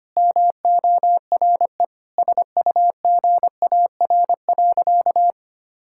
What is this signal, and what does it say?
Morse code, heard as a keyed tone, about 25 wpm, 700 Hz. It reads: MORE SUGAR.
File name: practice / 25wpm / Sentences with 4-Word Phrases - 25wpm